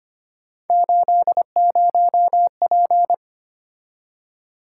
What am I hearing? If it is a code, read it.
Morse code, heard as a keyed tone, about 29 wpm, 700 Hz. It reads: 80P
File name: Speed-Racing - Three Letter-Number Combinations - 25wpm